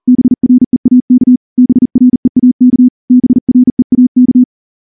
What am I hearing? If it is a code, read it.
Morse code, heard as a keyed tone, about 38 wpm, 250 Hz. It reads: BREAK BREAK BREAK
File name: morse-training-40WPM